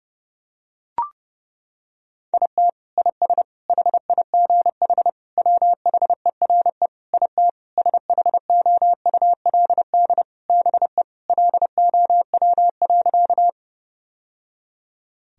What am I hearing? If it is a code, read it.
Morse code, heard as a keyed tone, about 30 wpm, 700 Hz. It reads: IT IS HIGH WHERE IT SHOULD BE LOW.
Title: Sentences from Top 1000 Words - Review Part 1 - 30wpm